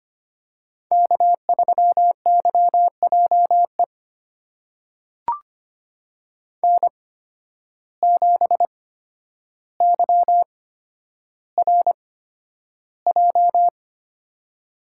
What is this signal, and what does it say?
Morse code, heard as a keyed tone, about 25 wpm, spaced as for 11 wpm, 700 Hz. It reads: K3YJE N7YRJ
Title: USA Call Signs - 5 Characters - 8x Character Spacing - 25wpm